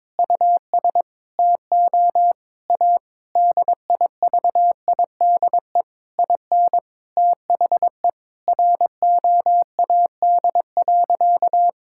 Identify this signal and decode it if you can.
Morse code, heard as a keyed tone, about 22 wpm, 700 Hz. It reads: US TO A DIVIDE IN THE ROAD.
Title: Sentences from Top 1600 Words 22wpm